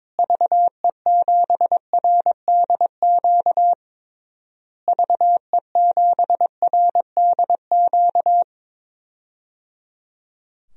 Morse code, heard as a keyed tone, about 22 wpm, 700 Hz. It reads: VE7RDQ VE7RDQ
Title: DXCC-100 Call Signs - 6 Characters - Repeated 3x - 22wpm